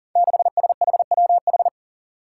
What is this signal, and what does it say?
Morse code, heard as a keyed tone, about 40 wpm, 700 Hz. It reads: 6SHWH
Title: Five Challenging Letter-Number Combinations 40wpm